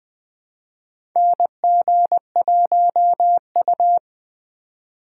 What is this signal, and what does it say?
Morse code, heard as a keyed tone, about 20 wpm, 700 Hz. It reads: NG1U
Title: USA Call Signs - 4 Characters - 3x Character Spacing - 20wpm